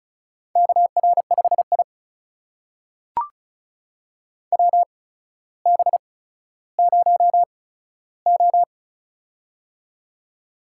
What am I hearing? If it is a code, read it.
Morse code, heard as a keyed tone, about 36 wpm, spaced as for 15 wpm, 700 Hz. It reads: KR5I WB0O
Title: USA Call Signs - 4 Characters - 8x Character Spacing - 35wpm